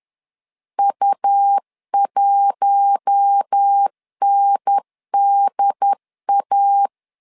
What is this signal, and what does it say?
Morse code, heard as a keyed tone, about 10 wpm, 800 Hz. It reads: U1NDA
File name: LEZIONI / 5 LBVC / 5 L 10wpm 2259